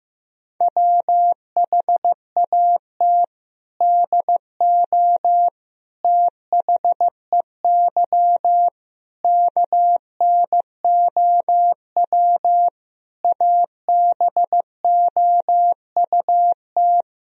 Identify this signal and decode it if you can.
Morse code, heard as a keyed tone, about 15 wpm, 700 Hz. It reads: WHAT DO THEY KNOW ABOUT
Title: Sentences from Top 1000 Words - Review Part 3 - 15wpm